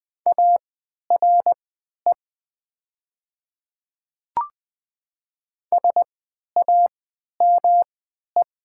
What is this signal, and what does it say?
Morse code, heard as a keyed tone, about 20 wpm, spaced as for 11 wpm, 700 Hz. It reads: ARE SAME
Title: 3x Character Spacing - Top 300 Words - 20wpm